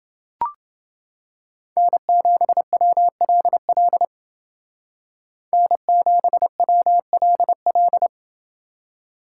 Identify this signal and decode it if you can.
Morse code, heard as a keyed tone, about 28 wpm, 700 Hz. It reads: N7WLL N7WLL